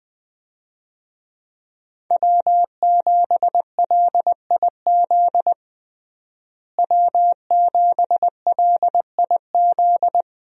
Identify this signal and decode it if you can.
Morse code, heard as a keyed tone, about 20 wpm, 700 Hz. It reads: W7LIZ W7LIZ
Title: USA Call Signs - 5 Characters - Repeated 2x - 20wpm